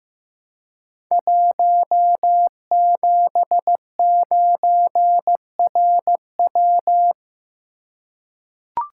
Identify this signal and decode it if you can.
Morse code, heard as a keyed tone, about 15 wpm, 700 Hz. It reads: 179RW
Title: Code Groups - 5 Letter:Numbers - 15wpm